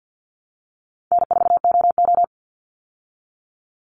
Morse code, heard as a keyed tone, about 50 wpm, 700 Hz. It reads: N4OO
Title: Easy Call Signs Repeated 3x - 50wpm